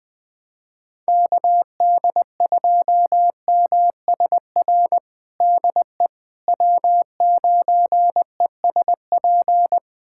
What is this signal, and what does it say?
Morse code, heard as a keyed tone, about 20 wpm, 700 Hz. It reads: KD2MSR DE W9ESP
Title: USA Callsign DE Callsign 20wpm